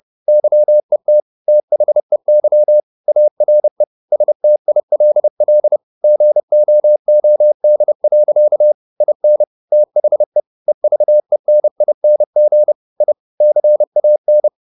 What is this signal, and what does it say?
Morse code, heard as a keyed tone, about 30 wpm, 600 Hz. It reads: YET THEY ARE STILL GOOD. IN THE EVENING I CAN
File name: the old man and the sea / the old man and the sea 1 30wpm